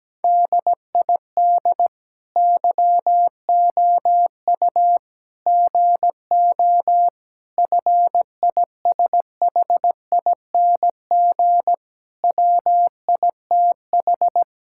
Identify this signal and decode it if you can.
Morse code, heard as a keyed tone, about 17 wpm, 700 Hz. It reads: DID YOU GO FISHING WITH
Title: Sentences from Top 1000 Words - Review Part 5 - 17wpm